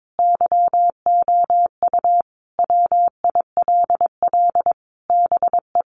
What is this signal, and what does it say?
Morse code, heard as a keyed tone, about 22 wpm, 700 Hz. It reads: YOU WILL BE
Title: Sentences from Top 1300 Words 22wpm